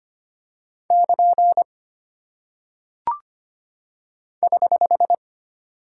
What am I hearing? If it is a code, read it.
Morse code, heard as a keyed tone, about 25 wpm, 700 Hz. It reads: <KN> <HH>
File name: Single Punctuation-Prosign - 25wpm